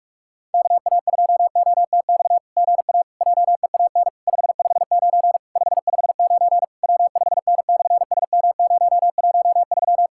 Morse code, heard as a keyed tone, about 46 wpm, 700 Hz. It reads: KA2OTX GA JEAN 559 559 W5N/SM013
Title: SOTA QSO Practice 45wpm